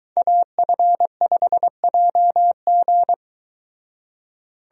Morse code, heard as a keyed tone, about 22 wpm, 700 Hz. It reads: AF5JG